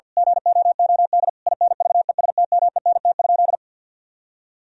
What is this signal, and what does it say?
Morse code, heard as a keyed tone, about 50 wpm, 700 Hz. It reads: GOOD INVESTMENT?
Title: Sentences from Top 1000 Words - Review Part 4 - 50wpm